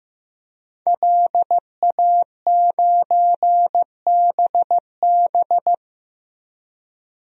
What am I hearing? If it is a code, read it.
Morse code, heard as a keyed tone, about 15 wpm, 700 Hz. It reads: LA9BB
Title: DXCC-100 Call Signs - 5 Characters - 15wpm